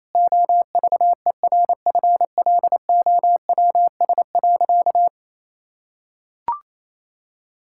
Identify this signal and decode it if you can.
Morse code, heard as a keyed tone, about 27 wpm, 700 Hz. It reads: OVERFLOWS.
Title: Pangram Challenge v2 - 28wpm